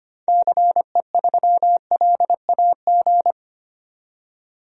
Morse code, heard as a keyed tone, about 25 wpm, 700 Hz. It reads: CE3LAG
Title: DXCC-100 Call Signs - 6 Characters - 25wpm